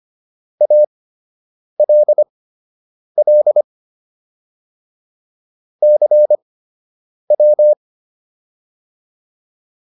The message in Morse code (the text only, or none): ALL CW